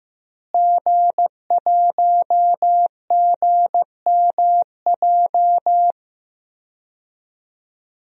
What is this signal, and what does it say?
Morse code, heard as a keyed tone, about 16 wpm, 700 Hz. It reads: G1GMJ
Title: DXCC-100 Call Signs - 5 Characters - Repeated 2x - 15wpm